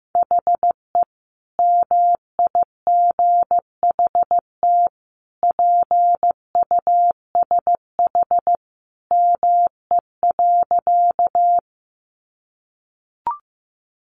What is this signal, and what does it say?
Morse code, heard as a keyed tone, about 15 wpm, 700 Hz. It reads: HE MIGHT PUSH ME.
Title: Sentences from Top 1300 Words 15wpm